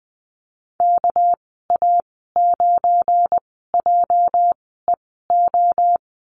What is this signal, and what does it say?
Morse code, heard as a keyed tone, about 20 wpm, spaced as for 14 wpm, 700 Hz. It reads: KA9JEO
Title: 207 Lesson 33g - Letter j - Call Signs 20wpm